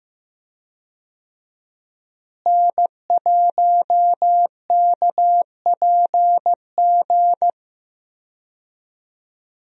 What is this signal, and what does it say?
Morse code, heard as a keyed tone, about 15 wpm, 700 Hz. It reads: N1KPG
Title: USA Call Signs - 5 Characters - Repeated 2x - 15wpm